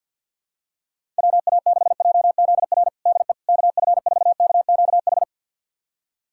Morse code, heard as a keyed tone, about 51 wpm, 700 Hz. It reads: WA6JZR DE KF4KQH
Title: USA Callsign DE Callsign - 6 Characters - 50wpm